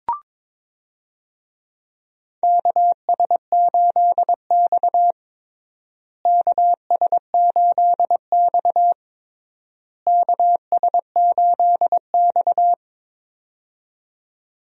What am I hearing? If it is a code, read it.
Morse code, heard as a keyed tone, about 22 wpm, 700 Hz. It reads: KS8X KS8X KS8X
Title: USA Call Signs - 4 Characters - Repeated 3x - 22wpm